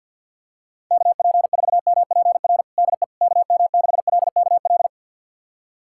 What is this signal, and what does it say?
Morse code, heard as a keyed tone, about 52 wpm, 700 Hz. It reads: KP4MPR DE KM6LKL
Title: USA Callsign DE Callsign - 6 Characters - 50wpm